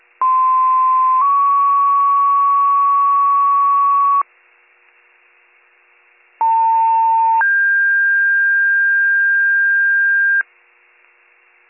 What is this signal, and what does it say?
Two-tone page, heard as a one-second tone, then a three-second tone, about 1040 then 1120 Hz, about 900 then 1600 Hz. It is heard from 0.2 to 10.4 s.